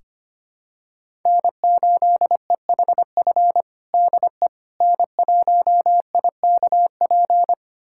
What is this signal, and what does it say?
Morse code, heard as a keyed tone, about 25 wpm, 700 Hz. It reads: N8EHF DE N1IKP
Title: USA Callsign DE Callsign - 5 Characters - 25wpm